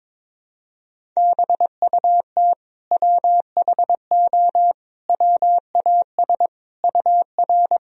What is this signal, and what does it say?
Morse code, heard as a keyed tone, about 22 wpm, 700 Hz. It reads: BUT WHO WAS UR
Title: Sentences from Top 200 Words and Abbreviations 22wpm